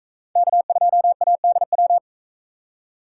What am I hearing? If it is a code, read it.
Morse code, heard as a keyed tone, about 41 wpm, 700 Hz. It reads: K2ADW